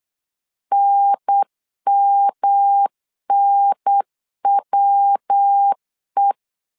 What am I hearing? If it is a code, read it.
Morse code, heard as a keyed tone, about 8 wpm, 800 Hz. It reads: NMNWE
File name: CW / DET / 1 3 1500C S B 8wpm 4141